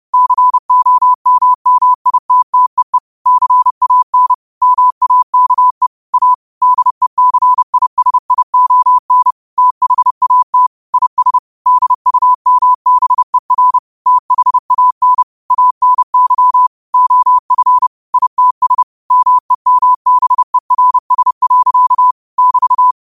Morse code, heard as a keyed tone, about 30 wpm, 1000 Hz. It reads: COMMITTEE CAN MAKE A DECISION THAT IS DUMBER THAN ANY OF ITS MEMBERS. <BT>
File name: qotdfiles / 30wpm